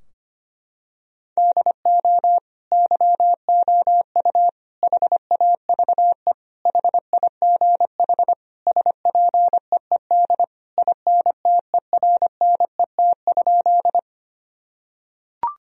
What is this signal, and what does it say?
Morse code, heard as a keyed tone, about 25 wpm, 700 Hz. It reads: DO YOU HAVE HIGH SPEED INTERNET?
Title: Sentences from Top 1000 Words - Review Part 4 - 25wpm